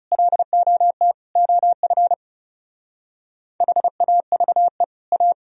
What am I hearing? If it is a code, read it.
Morse code, heard as a keyed tone, about 36 wpm, 700 Hz. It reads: LOT OF HAVE A
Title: Speed-Racing - Sets of 4 Words 30wpm